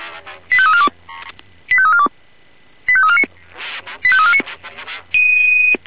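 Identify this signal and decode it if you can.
Five-tone selective call (ZVEI), heard as quick runs of five tones, tones of about 70 ms. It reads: 85342 85342 85349 85349
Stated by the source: Five-tone ZVEI selective call received on an SDR.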